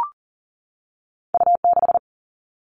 Morse code, heard as a keyed tone, about 42 wpm, 700 Hz. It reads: U 6
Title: Two Challenging Letter-Number Combinations 40wpm